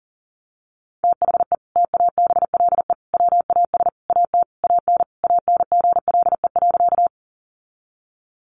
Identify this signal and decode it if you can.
Morse code, heard as a keyed tone, about 40 wpm, 700 Hz. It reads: THE TABLE WAS AT AN ANGLE.
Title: Sentences from Top 1500 Words 40wpm